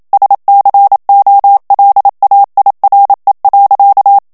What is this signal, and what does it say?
Morse code, heard as a keyed tone, about 27 wpm, 800 Hz. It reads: SCOLAIRE.
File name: CW / FICHIERS-MP3 / FRANCAIS TEXTE CLAIR 1 VITESSE 26 MOTS MINUTES